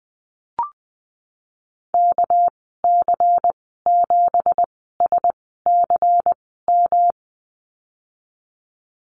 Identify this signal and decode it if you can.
Morse code, heard as a keyed tone, about 20 wpm, spaced as for 15 wpm, 700 Hz. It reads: KC7SCM
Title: 213 Lesson 34g - Number 8 - Call Signs 20wpm